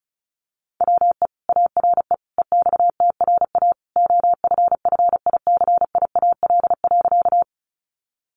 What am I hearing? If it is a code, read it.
Morse code, heard as a keyed tone, about 34 wpm, 700 Hz. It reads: WE ARE EXTRA OFFICIAL.
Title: Sentences from Top 1000 Words 35wpm